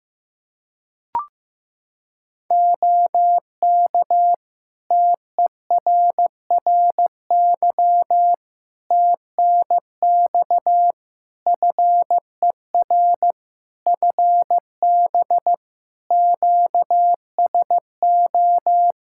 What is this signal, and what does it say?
Morse code, heard as a keyed tone, about 15 wpm, 700 Hz. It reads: OK TERRY TNX FER FB QSO
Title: Practice QSO - Exchange 4 - Ending - 15wpm